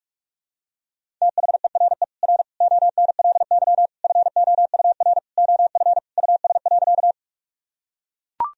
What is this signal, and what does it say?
Morse code, heard as a keyed tone, about 41 wpm, 700 Hz. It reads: THERE R ONLY FOUR OF US.
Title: Sentences from Top 300 Words and Abbreviations 45wpm